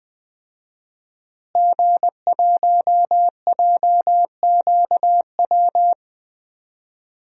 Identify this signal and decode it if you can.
Morse code, heard as a keyed tone, about 20 wpm, 700 Hz. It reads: G1JQW
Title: DXCC-100 Call Signs - 5 Characters - Repeated 3x - 20wpm